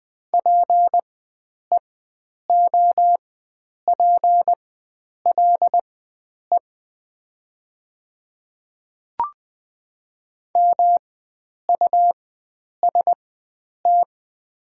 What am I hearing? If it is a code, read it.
Morse code, heard as a keyed tone, about 20 wpm, spaced as for 9 wpm, 700 Hz. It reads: PEOPLE MUST
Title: 4x Character Spacing - Top 300 Words - 20wpm